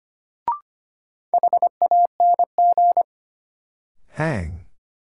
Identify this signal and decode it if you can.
Morse code, heard as a keyed tone, about 25 wpm, 700 Hz. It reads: HANG
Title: Sentences from Top 1600 Words - New Word Drill - 25wpm